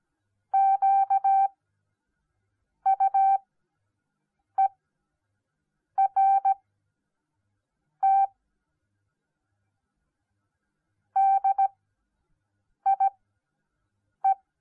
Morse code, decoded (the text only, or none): QUERT DIE